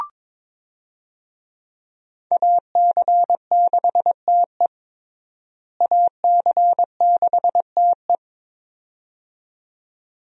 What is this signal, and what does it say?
Morse code, heard as a keyed tone, about 22 wpm, 700 Hz. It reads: AC6TE AC6TE
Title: USA Call Signs - 5 Characters - Repeated 2x - 22wpm